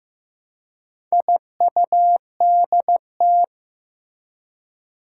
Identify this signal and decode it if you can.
Morse code, heard as a keyed tone, about 15 wpm, 700 Hz. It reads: IUDT
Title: Four Challenging Letter-Number Combinations 15wpm